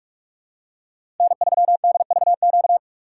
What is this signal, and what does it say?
Morse code, heard as a keyed tone, about 46 wpm, 700 Hz. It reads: N3DVQ
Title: USA Call Signs - 4x Character Spacing - 45wpm